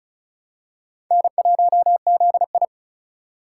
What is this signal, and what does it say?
Morse code, heard as a keyed tone, about 36 wpm, 700 Hz. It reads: N1ZI
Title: USA Call Signs - 4x Character Spacing - 35wpm